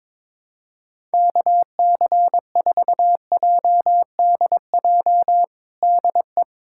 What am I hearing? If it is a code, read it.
Morse code, heard as a keyed tone, about 22 wpm, 700 Hz. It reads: KC4JDJ DE
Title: USA Callsign DE Callsign - 6 Characters - 22wpm